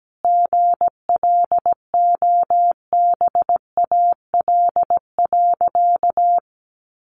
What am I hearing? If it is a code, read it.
Morse code, heard as a keyed tone, about 17 wpm, 700 Hz. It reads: GLOBAL.
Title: Sentences from Top 1200 Words 17wpm